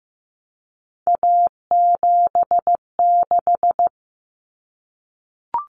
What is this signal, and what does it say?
Morse code, heard as a keyed tone, about 15 wpm, 700 Hz. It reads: A76
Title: Common Cut Numbers - 3-Digit 15wpm